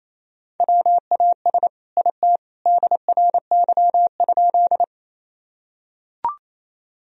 Morse code, heard as a keyed tone, about 28 wpm, 700 Hz. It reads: WAS IT DRY?
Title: Sentences from Top 1000 Words - Review Part 3 - 28wpm